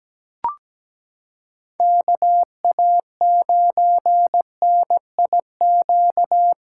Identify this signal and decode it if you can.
Morse code, heard as a keyed tone, about 17 wpm, 700 Hz. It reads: KA9NIQ
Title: USA Callsign DE Callsign - 6 Characters - 17wpm